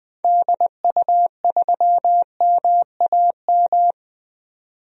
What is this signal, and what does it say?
Morse code, heard as a keyed tone, about 20 wpm, 700 Hz. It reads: DU3MAM
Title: DXCC-100 Call Signs - 6 Characters - 20wpm